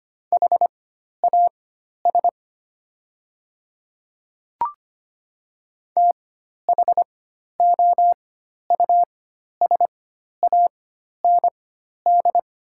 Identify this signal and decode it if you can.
Morse code, heard as a keyed tone, about 25 wpm, spaced as for 12 wpm, 700 Hz. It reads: HAS THOUSAND
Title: 4x Character Spacing - Top 500 Words - 25wpm